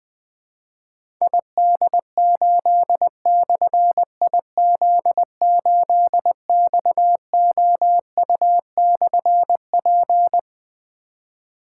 Morse code, heard as a keyed tone, about 20 wpm, 700 Hz. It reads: ID8/IZ8XOU/P
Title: Difficult Call Signs 20wpm